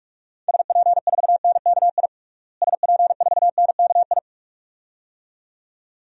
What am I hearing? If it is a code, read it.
Morse code, heard as a keyed tone, about 41 wpm, 700 Hz. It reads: SP4NKI SP4NKI
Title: DXCC-100 Call Signs - 6 Characters - Repeated 2x - 45wpm